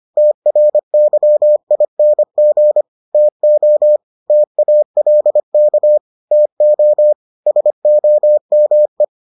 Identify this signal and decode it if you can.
Morse code, heard as a keyed tone, about 25 wpm, 600 Hz. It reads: TRYING TO TALK TO SOME